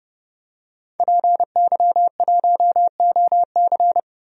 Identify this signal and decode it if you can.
Morse code, heard as a keyed tone, about 30 wpm, 700 Hz. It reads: PY1OC
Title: DXCC-100 Call Signs - 5 Characters - 30wpm